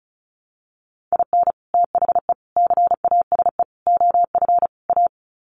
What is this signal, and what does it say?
Morse code, heard as a keyed tone, about 34 wpm, 700 Hz. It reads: IN THE CASE OF A
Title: Sets of 5 Words Encore 35wpm